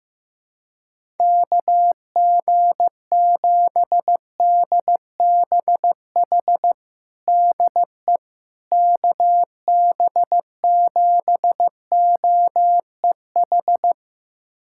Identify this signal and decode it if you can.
Morse code, heard as a keyed tone, about 15 wpm, 700 Hz. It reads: KG7DBH DE KB7OEH
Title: USA Callsign DE Callsign - 6 Characters - 15wpm